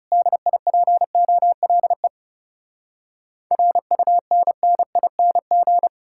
Morse code, heard as a keyed tone, about 37 wpm, 700 Hz. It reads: DIPOLE RUNNING
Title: Speed-Racing Common QSO Phrases - 30wpm